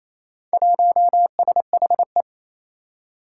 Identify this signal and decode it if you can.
Morse code, heard as a keyed tone, about 28 wpm, 700 Hz. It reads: 1SHE
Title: Four Challenging Letter-Number Combinations 28wpm